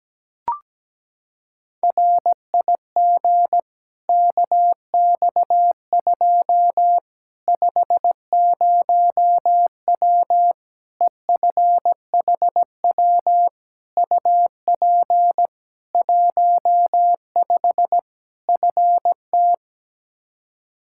Morse code, heard as a keyed tone, about 17 wpm, 700 Hz. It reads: RIG KX2 50W EFHW UP 15 FT
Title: K1USN SST QSO Practice 17wpm